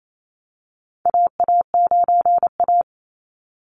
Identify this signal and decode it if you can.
Morse code, heard as a keyed tone, about 28 wpm, 700 Hz. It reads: AA9A
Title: Call Signs - 28wpm